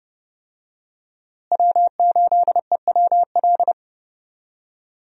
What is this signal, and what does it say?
Morse code, heard as a keyed tone, about 30 wpm, 700 Hz. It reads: W8EWL
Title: USA Call Signs - 5 Characters - Repeated 3x - 30wpm